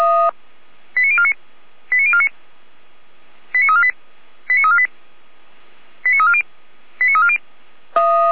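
Five-tone selective call (ZVEI), heard as quick runs of five tones, tones of about 70 ms. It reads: 89049 89049 89348 89348 89340 89340
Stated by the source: ZVEI five-tone selective call 89126 received by SDR.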